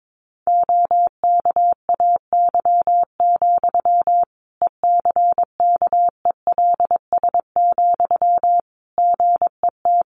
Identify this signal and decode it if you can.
Morse code, heard as a keyed tone, about 22 wpm, 700 Hz. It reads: OKAY, ECKELS, GET
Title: A Sound Of Thunder 22wpm